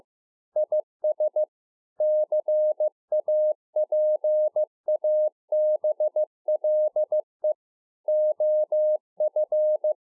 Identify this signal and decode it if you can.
Morse code, heard as a keyed tone, about 15 wpm, 610 Hz. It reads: IS CAPABLE OF